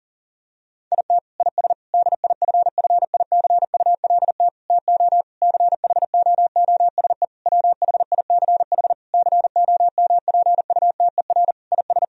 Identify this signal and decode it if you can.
Morse code, heard as a keyed tone, about 40 wpm, 700 Hz. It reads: IT IS DIFFICULT TO CHOOSE WHICH COMPUTER IS
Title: Sentences from Top 1000 Words - Review Part 3 - 40wpm